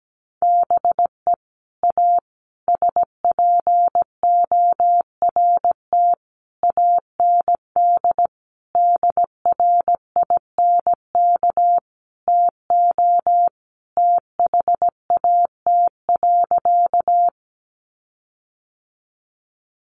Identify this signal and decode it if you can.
Morse code, heard as a keyed tone, about 17 wpm, 700 Hz. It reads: BE A SPORT AND DRINK TO THAT.
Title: Sentences from Top 1500 Words 17wpm